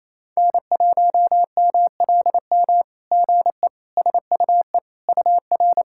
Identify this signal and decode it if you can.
Morse code, heard as a keyed tone, about 29 wpm, 660 Hz. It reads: N1MLM GE SUE UR
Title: SOTA QSO Practice 28wpm